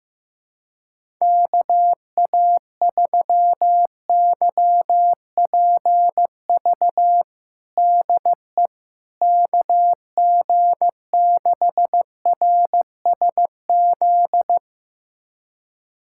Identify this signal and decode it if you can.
Morse code, heard as a keyed tone, about 15 wpm, 700 Hz. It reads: KA3YPV DE KG6RSZ
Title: USA Callsign DE Callsign 15wpm